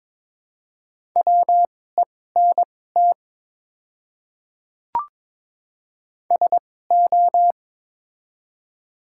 Morse code, heard as a keyed tone, about 22 wpm, spaced as for 16 wpm, 700 Hz. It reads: WENT SO
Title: 2x Character Spacing - Top 300 Words - 22wpm